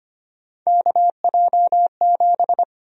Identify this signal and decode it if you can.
Morse code, heard as a keyed tone, about 25 wpm, 700 Hz. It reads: KJ7
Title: USA Call Sign Prefixes - 25wpm